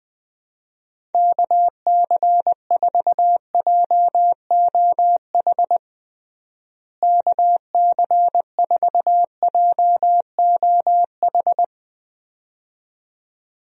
Morse code, heard as a keyed tone, about 20 wpm, 700 Hz. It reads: KC4JOH KC4JOH
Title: USA Call Signs - 6 Characters - Repeated 2x - 20wpm